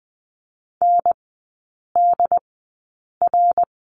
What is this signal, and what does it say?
Morse code, heard as a keyed tone, about 20 wpm, spaced as for 8 wpm, 700 Hz. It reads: NDR
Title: 067 Lesson 12e - Letter d - Three Characters 20wpm